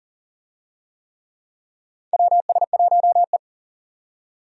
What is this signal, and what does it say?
Morse code, heard as a keyed tone, about 40 wpm, 700 Hz. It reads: WS1E
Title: USA Call Signs - 4 Characters - 40wpm